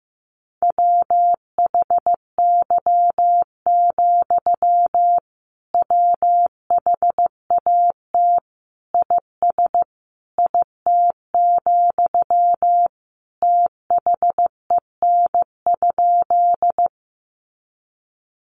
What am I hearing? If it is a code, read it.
Morse code, heard as a keyed tone, about 15 wpm, 700 Hz. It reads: WHY, WHAT IS IT, THEN?
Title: Sentences from Top 200 Words 15wpm